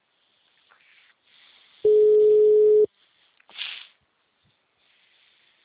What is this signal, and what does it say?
Ringback tone (Europe, 425 Hz).